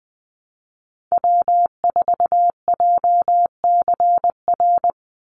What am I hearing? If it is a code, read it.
Morse code, heard as a keyed tone, about 20 wpm, 700 Hz. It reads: W4JCR
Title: USA Call Signs 20wpm